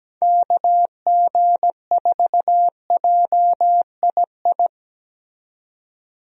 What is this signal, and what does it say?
Morse code, heard as a keyed tone, about 17 wpm, 700 Hz. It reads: KG4JII
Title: USA Call Signs - 6 Characters - 17wpm